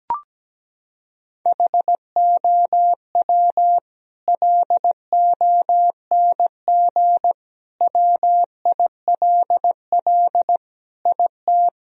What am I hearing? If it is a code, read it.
Morse code, heard as a keyed tone, about 17 wpm, 700 Hz. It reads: HOW LONG WILL IT
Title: Sentences from Top 1000 Words - Review Part 2 - 17wpm